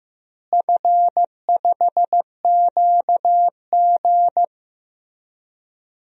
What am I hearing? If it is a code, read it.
Morse code, heard as a keyed tone, about 15 wpm, 700 Hz. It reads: F5QG
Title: DXCC-100 Call Signs - 4 Characters - Repeated 2x - 15wpm